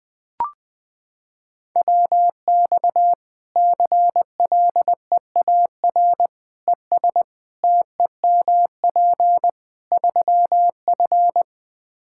Morse code, heard as a keyed tone, about 21 wpm, 700 Hz. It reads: WX CLEAR ES TEMP 3F
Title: Practice QSO - Exchange 2 - Info - 20wpm